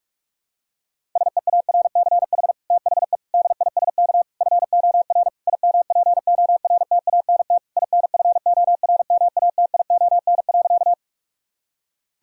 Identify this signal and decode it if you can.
Morse code, heard as a keyed tone, about 42 wpm, 700 Hz. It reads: SEARCH THE DISK FOR IMPORTANT INFORMATION.
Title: Sentences from Top 1000 Words - Review Part 6 - 45wpm